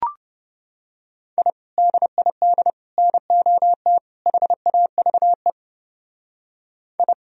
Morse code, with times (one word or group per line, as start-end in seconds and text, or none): 1.4-1.5 I
1.8-2.7 DID
3.0-4.0 NOT
4.3-5.5 HAVE
7.0-7.1 I